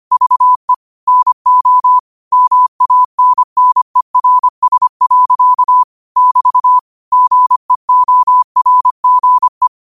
Morse code, heard as a keyed tone, about 25 wpm, 1000 Hz. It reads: VE NO MANNERS. <BT> GEORGE
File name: qotdfiles / 25wpm